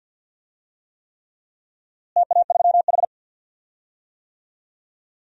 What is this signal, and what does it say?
Morse code, heard as a keyed tone, about 53 wpm, 700 Hz. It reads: TA3H
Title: DXCC-100 Call Signs - 4 Characters - Repeated 3x - 50wpm